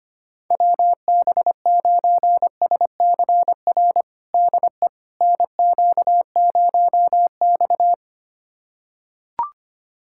Morse code, heard as a keyed tone, about 25 wpm, 700 Hz. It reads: WB9SCR DE NQ0X
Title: USA Callsign DE Callsign 25wpm